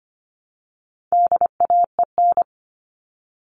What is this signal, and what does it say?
Morse code, heard as a keyed tone, about 25 wpm, 700 Hz. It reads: DAEN
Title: All Cut Numbers - 4-Digit 25wpm